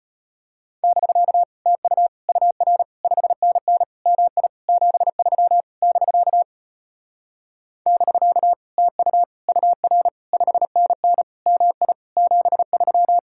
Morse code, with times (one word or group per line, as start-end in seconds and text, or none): 0.8-1.4 <BK>
1.7-2.1 TU
2.3-2.8 UR
3.0-3.8 5NN
4.1-4.5 MI
4.7-5.6 73
5.8-6.4 <BK>
7.9-8.5 <BK>
8.8-9.2 TU
9.5-10.1 UR
10.3-11.2 5NN
11.5-11.9 MI
12.2-13.2 73